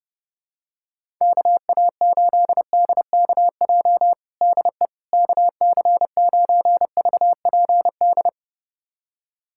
Morse code, heard as a keyed tone, about 30 wpm, 700 Hz. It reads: KA8DKJ DE KC9VPD
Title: USA Callsign DE Callsign - 6 Characters - 30wpm